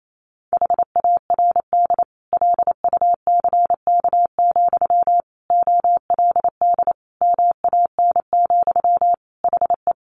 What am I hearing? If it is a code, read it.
Morse code, heard as a keyed tone, about 28 wpm, 700 Hz. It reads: HARD LUCK, OLD MAN, HE
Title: Sentences from Top 1500 Words 28wpm